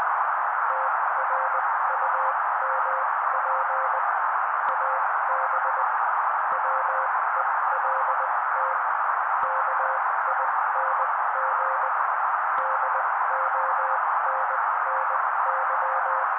Morse code, heard as a keyed tone, about 20 wpm, spaced as for 16 wpm, 530 Hz. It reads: TRUMP AB WELT KING DONNY
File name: daily / 16wpm noise